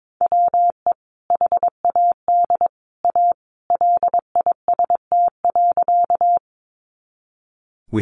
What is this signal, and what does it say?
Morse code, heard as a keyed tone, about 22 wpm, 700 Hz. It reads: WE HAD A LIST.
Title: Sentences from Top 400 Words 22wpm